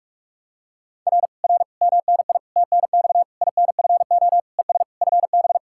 Morse code, heard as a keyed tone, about 40 wpm, 700 Hz. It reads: R R MNI TNX INFO ES FB